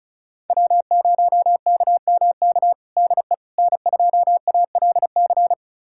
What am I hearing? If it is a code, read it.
Morse code, heard as a keyed tone, about 36 wpm, 700 Hz. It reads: W0KMK DE N2ALC
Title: USA Callsign DE Callsign - 5 Characters - 35wpm